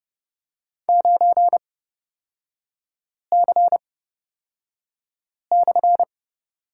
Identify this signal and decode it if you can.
Morse code, heard as a keyed tone, about 31 wpm, spaced as for 5 wpm, 690 Hz. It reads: 9C/
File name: Lesson 37e - Letter x - Three Characters 30wpm